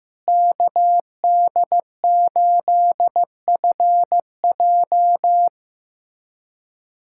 Morse code, heard as a keyed tone, about 16 wpm, 700 Hz. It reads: KD8FJ
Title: SOTA QSO Practice 15wpm